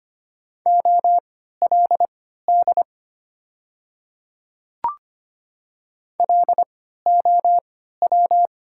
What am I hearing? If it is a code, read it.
Morse code, heard as a keyed tone, about 25 wpm, spaced as for 14 wpm, 690 Hz. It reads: OLD LOW
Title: 3x Character Spacing - Top 300 Words - 25wpm